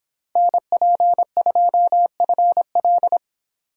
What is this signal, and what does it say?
Morse code, heard as a keyed tone, about 26 wpm, 700 Hz. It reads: NP2FL